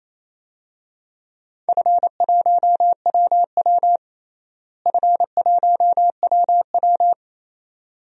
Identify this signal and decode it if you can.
Morse code, heard as a keyed tone, about 28 wpm, 700 Hz. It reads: F1WW F1WW
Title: DXCC-100 Call Signs - 4 Characters - Repeated 2x - 28wpm